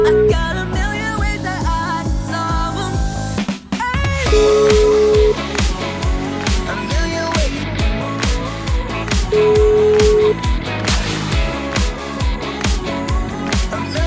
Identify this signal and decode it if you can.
Ringback tone (Europe, 425 Hz), with noise.